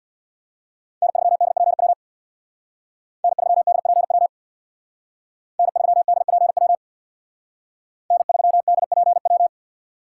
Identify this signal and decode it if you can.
Morse code, heard as a keyed tone, about 63 wpm, 700 Hz. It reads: N3DPW N3DPW N3DPW N3DPW